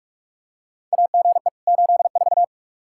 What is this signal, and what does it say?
Morse code, heard as a keyed tone, about 47 wpm, 700 Hz. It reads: AGE 84
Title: Practice QSO - Exchange 3 - More Info - 45wpm